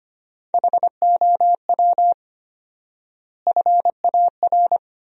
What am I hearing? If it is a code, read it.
Morse code, heard as a keyed tone, about 25 wpm, 700 Hz. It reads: HOW FAR
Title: 4x Word-Spacing - Sentences from Top 400 Words - 25wpm